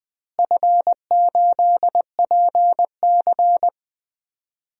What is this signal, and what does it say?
Morse code, heard as a keyed tone, about 20 wpm, 700 Hz. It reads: F8PC
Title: DXCC-100 Call Signs - 4 Characters - Repeated 3x - 20wpm